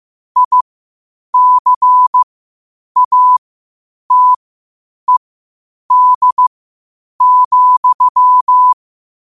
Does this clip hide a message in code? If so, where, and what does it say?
Yes: Morse code from 0.4 to 8.7 s, heard as a keyed tone, about 15 wpm, spaced as for 8 wpm, 1000 Hz. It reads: ICATED,